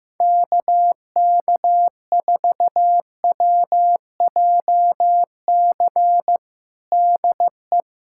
Morse code, heard as a keyed tone, about 15 wpm, 700 Hz. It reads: KK4WJC DE
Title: USA Callsign DE Callsign - 6 Characters - 15wpm